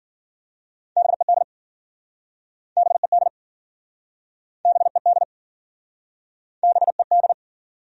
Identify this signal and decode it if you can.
Morse code, heard as a keyed tone, about 56 wpm, 700 Hz. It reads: BED BED BED BED